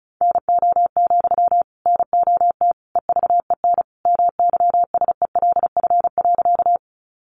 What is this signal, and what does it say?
Morse code, heard as a keyed tone, about 34 wpm, 700 Hz. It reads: NO, NOT EVEN MYSELF.
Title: Sentences from Top 1300 Words 35wpm